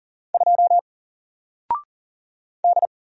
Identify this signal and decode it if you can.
Morse code, heard as a keyed tone, about 40 wpm, 700 Hz. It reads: 2 D
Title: Single Letter-Number Alt 40wpm